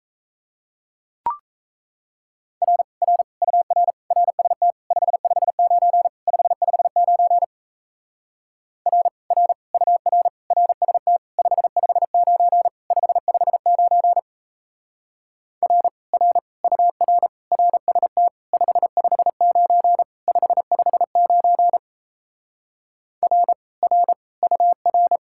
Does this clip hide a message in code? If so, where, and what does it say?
Yes: Morse code from 2.6 to 25.2 s, heard as a keyed tone, about 37 wpm, 700 Hz. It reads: R R UR RST 559 559 R R UR RST 559 559 R R UR RST 559 559 R R UR